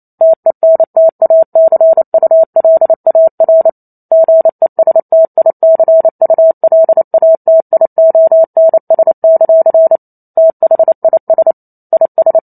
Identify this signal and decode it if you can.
Morse code, heard as a keyed tone, about 28 wpm, 650 Hz. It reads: TENTACULAR GESTICULATIONS; THIS IS